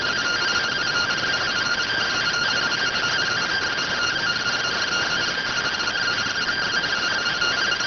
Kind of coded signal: two-tone teleprinter signal

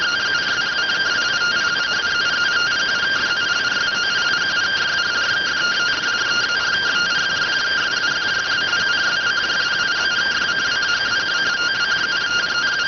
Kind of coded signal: two-tone teleprinter signal

